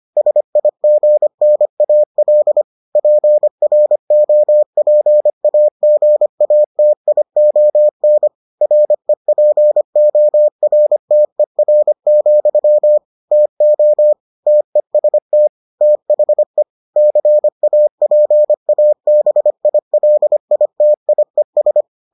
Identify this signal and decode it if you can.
Morse code, heard as a keyed tone, about 25 wpm, 600 Hz. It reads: SIGNAL PROPAGATION REPORTER, TO TEST THE CAPABILITIES